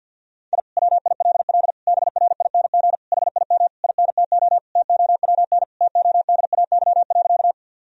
Morse code, heard as a keyed tone, about 50 wpm, 700 Hz. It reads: I WILL BRING HIM INTO TOWN TODAY.
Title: Sentences from Top 1000 Words - Review Part 3 - 50wpm